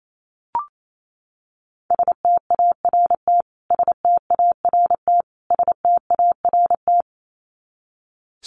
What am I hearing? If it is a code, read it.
Morse code, heard as a keyed tone, about 27 wpm, 700 Hz. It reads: START START START
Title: Top 500 Words Repeated 28wpm